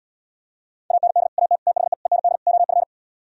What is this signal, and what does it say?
Morse code, heard as a keyed tone, about 77 wpm, 700 Hz. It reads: AND AT THE END OF